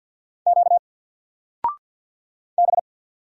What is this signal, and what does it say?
Morse code, heard as a keyed tone, about 50 wpm, 700 Hz. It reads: Q B
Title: Single Letter-Number 50wpm